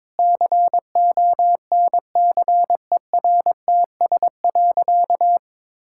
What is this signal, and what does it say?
Morse code, heard as a keyed tone, about 22 wpm, 700 Hz. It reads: CONCERTS.
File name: practice / 22wpm / Sentences with 8-Word Phrases - 22wpm